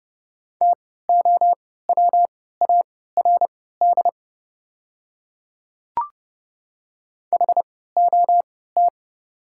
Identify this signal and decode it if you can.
Morse code, heard as a keyed tone, about 30 wpm, spaced as for 17 wpm, 700 Hz. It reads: TOWARD HOT
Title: 3x Character Spacing - Top 500 Words - 30wpm